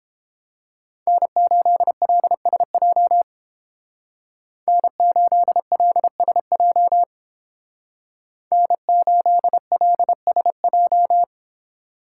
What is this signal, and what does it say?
Morse code, heard as a keyed tone, about 31 wpm, 700 Hz. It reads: N8LSJ N8LSJ N8LSJ